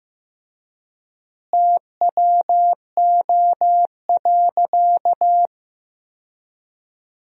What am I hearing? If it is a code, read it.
Morse code, heard as a keyed tone, about 15 wpm, 700 Hz. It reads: TWO.
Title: 3x Word-Spacing - Sentences from Top 300 Words - 15wpm